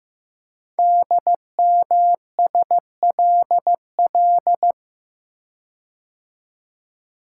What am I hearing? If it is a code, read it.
Morse code, heard as a keyed tone, about 15 wpm, 700 Hz. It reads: DMSLL
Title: Code Groups - 5 Letter:Numbers - 15wpm